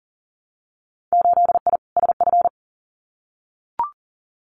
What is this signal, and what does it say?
Morse code, heard as a keyed tone, about 40 wpm, 700 Hz. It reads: 8I SF
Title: Winter Field Day Exchanges Encore 40wpm